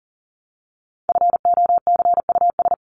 Morse code, heard as a keyed tone, about 40 wpm, 700 Hz. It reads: FOCUS
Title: Top 501-750 Words 40wpm